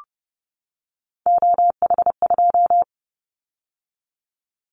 Morse code, heard as a keyed tone, about 30 wpm, 700 Hz. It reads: OH2
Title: Callsign Prefixes for Top 25 DXCC - 30wpm